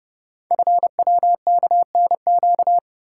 Morse code, heard as a keyed tone, about 30 wpm, 700 Hz. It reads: FWKNQ
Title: Code Groups - 5 Letters - 30wpm